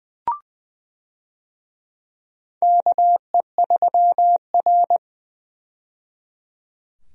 Morse code, heard as a keyed tone, about 20 wpm, 700 Hz. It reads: KE3R
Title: USA Call Signs - 4 Characters - 20wpm